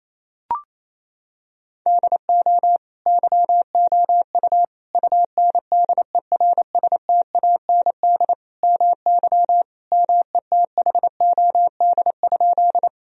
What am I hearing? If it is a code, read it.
Morse code, heard as a keyed tone, about 28 wpm, 700 Hz. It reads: DO YOU UNDERSTAND MY METHOD?
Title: Sentences from Top 1000 Words - Review Part 3 - 28wpm